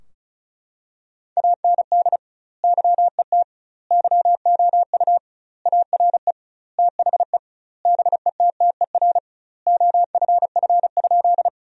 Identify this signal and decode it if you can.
Morse code, heard as a keyed tone, about 36 wpm, 700 Hz. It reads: AND YET YOU ARE THE BETTER OFF?
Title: 2x Word-Spacing - Sentences from Top 500 Words - 35wpm